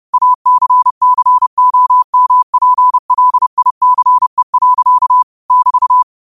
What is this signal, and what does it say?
Morse code, heard as a keyed tone, about 30 wpm, 1000 Hz. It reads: ACCOMPLICE. <BT>
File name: qotdfiles / 30wpm